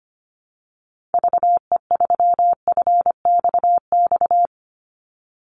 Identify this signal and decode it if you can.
Morse code, heard as a keyed tone, about 25 wpm, 700 Hz. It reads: VE3FXX
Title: Call Signs - 25wpm